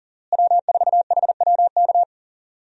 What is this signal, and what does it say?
Morse code, heard as a keyed tone, about 40 wpm, 700 Hz. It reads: W4HWK
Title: USA Call Signs - 5 Characters - Repeated 2x - 40wpm